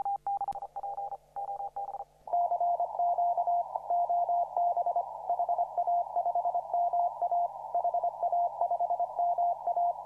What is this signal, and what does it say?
Morse code, heard as a keyed tone, about 26 wpm, 700 Hz, with noise. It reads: CQ OB HA5MA HA5MA
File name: CW / RHOB CW 2017 HA6P 0001